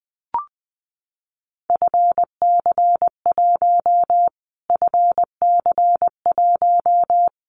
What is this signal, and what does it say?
Morse code, heard as a keyed tone, about 20 wpm, 700 Hz. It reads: FC1 FC1
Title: Callsign Prefixes for Top 25 DXCC - Repeated 3x - 20wpm